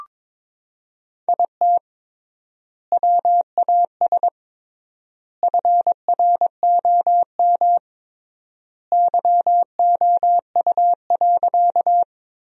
Morse code, heard as a keyed tone, about 22 wpm, 700 Hz. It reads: IT WAS FROM YOU.
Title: 3x Word-Spacing - Sentences from Top 100 Words - 22wpm